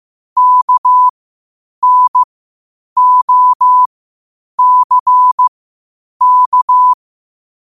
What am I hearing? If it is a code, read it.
Morse code, heard as a keyed tone, about 15 wpm, spaced as for 8 wpm, 1000 Hz. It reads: KNOCK
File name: qotdfiles / 8wpm